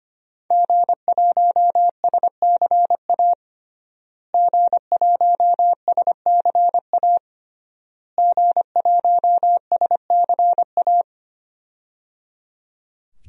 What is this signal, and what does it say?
Morse code, heard as a keyed tone, about 25 wpm, 700 Hz. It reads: G1SCA G1SCA G1SCA
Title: DXCC-100 Call Signs - 5 Characters - Repeated 3x - 25wpm